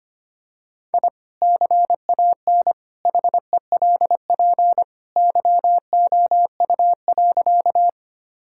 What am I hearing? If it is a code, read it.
Morse code, heard as a keyed tone, about 25 wpm, 700 Hz. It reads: I CAN HELP YOU.
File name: practice / 25wpm / Sentences with 2-Word Phrases - 25wpm